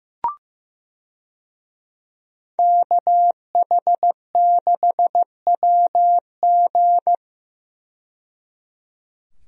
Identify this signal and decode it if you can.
Morse code, heard as a keyed tone, about 15 wpm, 700 Hz. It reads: KH6WG
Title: DXCC-100 Call Signs - 5 Characters - 15wpm